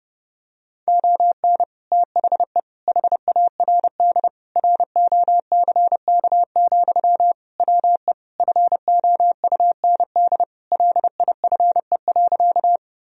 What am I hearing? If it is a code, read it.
Morse code, heard as a keyed tone, about 30 wpm, 700 Hz. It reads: ON THE HARD ROCK, WE FOUND LIFE.
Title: Sentences from Top 1000 Words - Review Part 2 - 30wpm